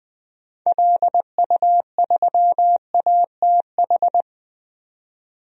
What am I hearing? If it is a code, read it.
Morse code, heard as a keyed tone, about 20 wpm, 700 Hz. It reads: LU3ATH
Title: DXCC-100 Call Signs - 6 Characters - 20wpm